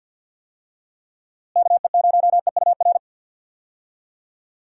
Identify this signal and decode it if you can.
Morse code, heard as a keyed tone, about 51 wpm, 700 Hz. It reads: KE0EUR
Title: USA Call Signs - 6 Characters - Repeated 2x - 50wpm